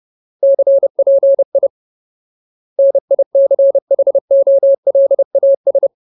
Morse code, heard as a keyed tone, about 30 wpm, 550 Hz. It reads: CPI NICHOLAS
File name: Ultimate Word List / 30WPM Ultimate